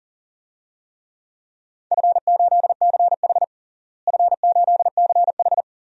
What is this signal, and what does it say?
Morse code, heard as a keyed tone, about 40 wpm, 700 Hz. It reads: F8CH F8CH
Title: DXCC-100 Call Signs - 4 Characters - Repeated 2x - 40wpm